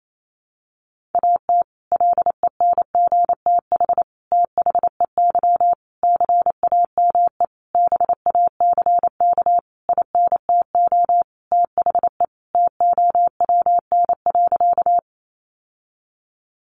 Morse code, heard as a keyed tone, about 28 wpm, 700 Hz. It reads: AT LENGTH THEY CAME BACK INTO THE TOWN.
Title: Sentences from Top 1000 Words 28wpm